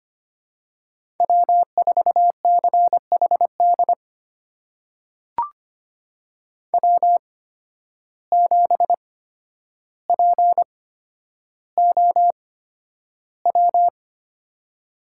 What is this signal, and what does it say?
Morse code, heard as a keyed tone, about 25 wpm, spaced as for 11 wpm, 700 Hz. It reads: W4CHD W7POW
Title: USA Call Signs - 5 Characters - 8x Character Spacing - 25wpm